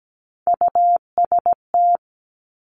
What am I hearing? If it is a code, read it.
Morse code, heard as a keyed tone, about 17 wpm, 700 Hz. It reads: UST
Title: Top Three Letter Combinations in Words 17wpm